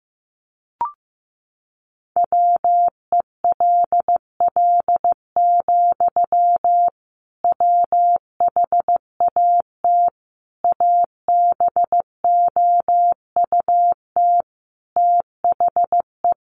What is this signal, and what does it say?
Morse code, heard as a keyed tone, about 15 wpm, 700 Hz. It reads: WELL, WHAT ABOUT THE
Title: Sentences from Top 1200 Words 15wpm